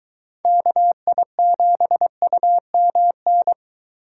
Morse code, heard as a keyed tone, about 23 wpm, 700 Hz. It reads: KI7UMN